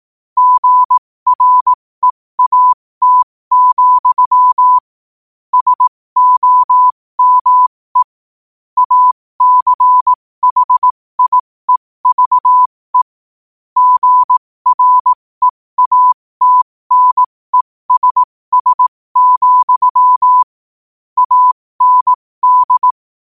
Morse code, heard as a keyed tone, about 18 wpm, spaced as for 15 wpm, 1000 Hz. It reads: GREAT, SOME ACHIEVE GREATNESS, AND
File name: qotdfiles / 15wpm